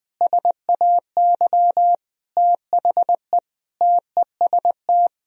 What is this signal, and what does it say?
Morse code, heard as a keyed tone, about 21 wpm, 700 Hz. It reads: SAY THE TEST
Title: Sentences from Top 1000 Words - Review Part 3 - 20wpm